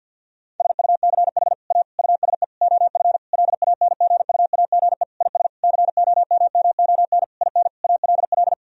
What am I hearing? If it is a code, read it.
Morse code, heard as a keyed tone, about 50 wpm, 700 Hz. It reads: SUCH A USE OF LANGUAGE IS COMMON IN ALL
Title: Sentences from Top 1000 Words - Review Part 3 - 50wpm